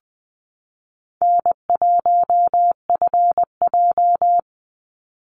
Morse code, heard as a keyed tone, about 20 wpm, 700 Hz. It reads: N1FJ
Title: NA SOTA Chaser Callsigns 20wpm